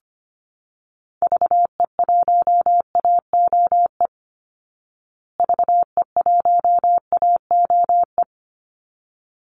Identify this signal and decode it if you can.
Morse code, heard as a keyed tone, about 25 wpm, 700 Hz. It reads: VE1AOE VE1AOE
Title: Call Signs Repeated 3x - 25wpm